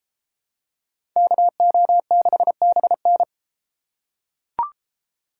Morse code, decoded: KO6BN

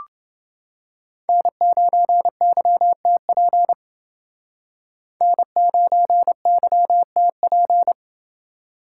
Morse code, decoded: N9YTP N9YTP

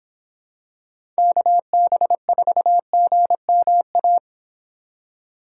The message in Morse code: KB4GMA

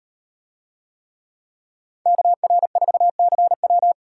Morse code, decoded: KR4CW

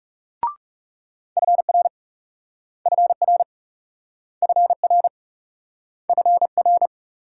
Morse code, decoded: FR FR FR FR

